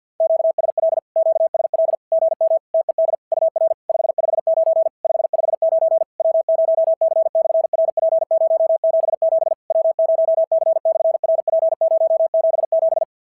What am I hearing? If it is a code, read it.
Morse code, heard as a keyed tone, about 49 wpm, 690 Hz. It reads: QSL QSL GM TED UR 559 559 W0C/RP077 W0C/RP077
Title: SOTA QSO Practice 50wpm